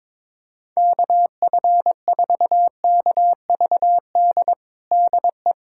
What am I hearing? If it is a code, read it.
Morse code, heard as a keyed tone, about 22 wpm, 700 Hz. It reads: KF4KVD DE
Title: USA Callsign DE Callsign - 6 Characters - 22wpm